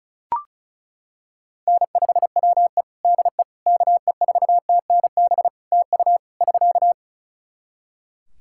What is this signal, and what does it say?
Morse code, heard as a keyed tone, about 34 wpm, 700 Hz. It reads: N5WE DE KE4TNB TU <SK>
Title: K1USN SST QSO Practice 35wpm